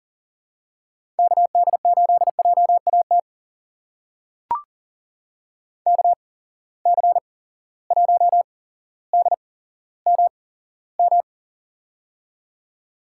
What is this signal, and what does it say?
Morse code, heard as a keyed tone, about 40 wpm, spaced as for 17 wpm, 700 Hz. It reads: KD8JAT KC1DMM
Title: USA Call Signs - 6 Characters - 8x Character Spacing - 40wpm